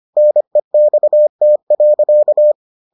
Morse code, heard as a keyed tone, about 25 wpm, 600 Hz. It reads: NEXT.